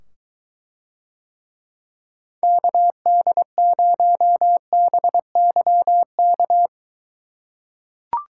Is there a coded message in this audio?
Yes: Morse code from 2.4 to 6.7 s, heard as a keyed tone, about 21 wpm, 700 Hz. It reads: KD0BYK